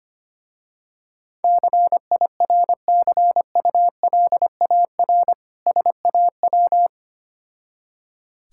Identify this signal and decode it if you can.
Morse code, heard as a keyed tone, about 25 wpm, 700 Hz. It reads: CIRCULAR SAW
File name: practice / 25wpm / Difficult Words with Hints - 25wpm